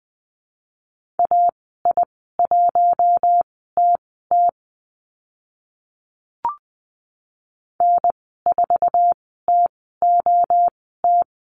Morse code, heard as a keyed tone, about 20 wpm, spaced as for 14 wpm, 700 Hz. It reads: AI1TT N4TOT
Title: 062 Lesson 11g - Letter h - Call Signs 20wpm